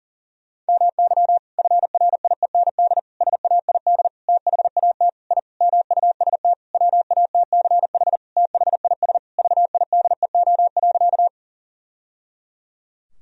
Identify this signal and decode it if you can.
Morse code, heard as a keyed tone, about 38 wpm, 700 Hz. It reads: MY FRIEND SAID THAT I MUST WATCH THIS VIDEO.
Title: Sentences from Top 1000 Words - Review Part 4 - 40wpm